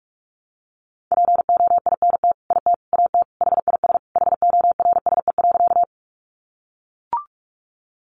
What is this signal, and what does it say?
Morse code, heard as a keyed tone, about 44 wpm, 700 Hz. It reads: POINT IT AT HIS HORSE.
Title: Sentences from Top 300 Words 45wpm